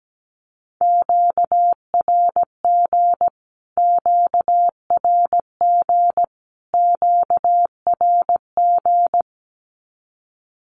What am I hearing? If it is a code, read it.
Morse code, heard as a keyed tone, about 17 wpm, 700 Hz. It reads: QRG QRG QRG
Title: Q-Codes Repeated 3x 17wpm